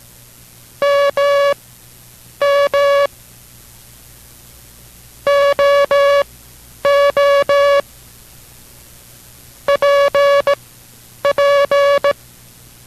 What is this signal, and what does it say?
Morse code, heard as a keyed tone, about 18 wpm, spaced as for 9 wpm, 570 Hz. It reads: MM OO PP